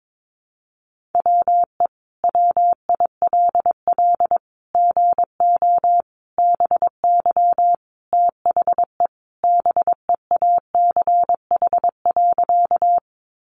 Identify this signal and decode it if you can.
Morse code, heard as a keyed tone, about 22 wpm, 700 Hz. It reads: WE WILL GO BY THE BEACH.
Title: Sentences from Top 1500 Words 22wpm